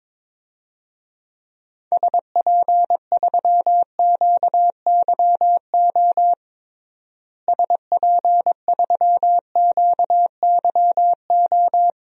Morse code, heard as a keyed tone, about 22 wpm, 700 Hz. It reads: SP3QYO SP3QYO
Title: DXCC-100 Call Signs - 6 Characters - Repeated 3x - 22wpm